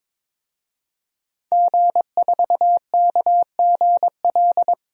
Morse code, heard as a keyed tone, about 21 wpm, 700 Hz. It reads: G4KGL
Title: DXCC-100 Call Signs - 5 Characters - 22wpm